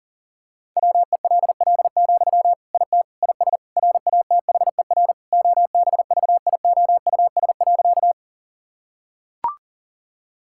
Morse code, heard as a keyed tone, about 40 wpm, 700 Hz. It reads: WELL, IT IS RATHER OBVIOUS.
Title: Sentences from Top 1600 Words 40wpm